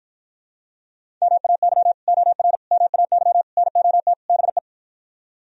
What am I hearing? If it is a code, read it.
Morse code, heard as a keyed tone, about 52 wpm, 700 Hz. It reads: MAY OR MAY NOT BE